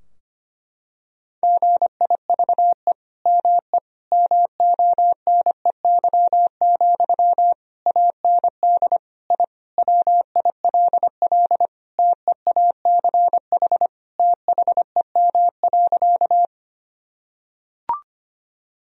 Morse code, heard as a keyed tone, about 25 wpm, 700 Hz. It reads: GIVE ME MONEY, AND I WILL TEACH THEM.
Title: Sentences from Top 1000 Words - Review Part 2 - 25wpm